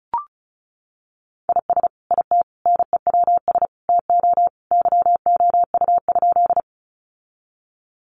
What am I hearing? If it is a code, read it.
Morse code, heard as a keyed tone, about 34 wpm, 700 Hz. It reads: IS IT NEWS TO YOU?
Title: Sentences from Top 800 Words 35wpm